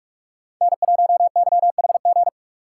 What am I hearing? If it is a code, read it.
Morse code, heard as a keyed tone, about 45 wpm, 690 Hz. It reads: N1YHG
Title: USA Call Signs - 5 Characters - 6x Character Spacing - 45wpm